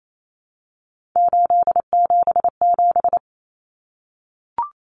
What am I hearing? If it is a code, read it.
Morse code, heard as a keyed tone, about 28 wpm, 700 Hz. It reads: 877
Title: Common Cut Numbers - 3-Digit 28wpm